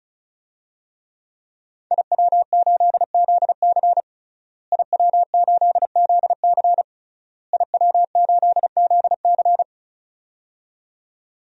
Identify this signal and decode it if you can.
Morse code, heard as a keyed tone, about 36 wpm, 700 Hz. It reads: IW8ZC IW8ZC IW8ZC
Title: DXCC-100 Call Signs - 5 Characters - Repeated 3x - 35wpm